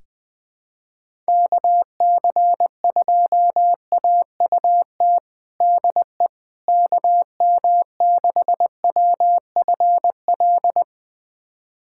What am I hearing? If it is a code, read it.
Morse code, heard as a keyed tone, about 21 wpm, 700 Hz. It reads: KC2AUT DE KM6WFL
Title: USA Callsign DE Callsign - 6 Characters - 20wpm